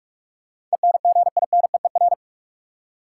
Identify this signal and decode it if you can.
Morse code, heard as a keyed tone, about 42 wpm, 700 Hz. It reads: ENGINEER
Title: Practice QSO - Exchange 3 - More Info - 45wpm